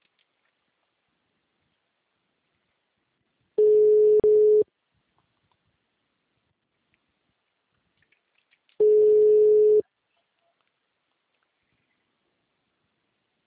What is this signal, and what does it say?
Ringback tone (Europe, 425 Hz).